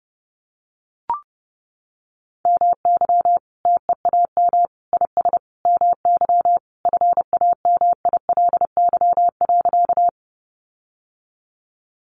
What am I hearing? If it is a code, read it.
Morse code, heard as a keyed tone, about 30 wpm, 700 Hz. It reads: MY TEAM IS MY FAMILY.
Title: Sentences from Top 1500 Words 30wpm